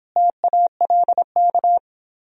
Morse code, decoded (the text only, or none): TALK